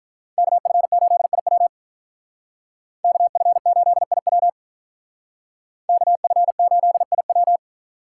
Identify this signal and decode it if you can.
Morse code, heard as a keyed tone, about 44 wpm, 700 Hz. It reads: KF8IW KF8IW KF8IW